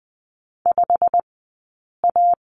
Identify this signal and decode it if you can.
Morse code, heard as a keyed tone, about 20 wpm, spaced as for 8 wpm, 700 Hz. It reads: 5A
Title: 165 Lesson 27d - Number 7 - Two Characters 20wpm